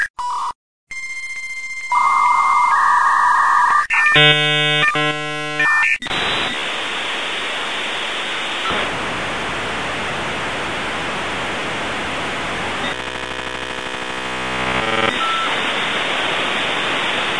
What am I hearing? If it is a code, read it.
Dial-up modem handshake.
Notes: un modem a 32k che faceva un caratteristico rumore ogni volta che ci si connetteva alla rete